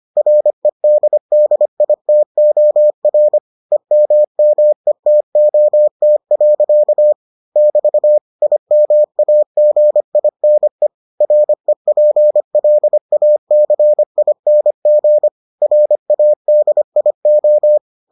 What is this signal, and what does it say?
Morse code, heard as a keyed tone, about 25 wpm, 600 Hz. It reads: REDDITOR EMMETOT. <BT> IMAGINE REPLACING RADIO